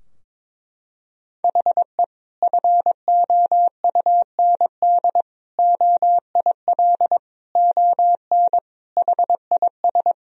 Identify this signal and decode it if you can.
Morse code, heard as a keyed tone, about 22 wpm, 700 Hz. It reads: HE FOUND OIL ON HIS
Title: Sentences from Top 1000 Words - Review Part 4 - 22wpm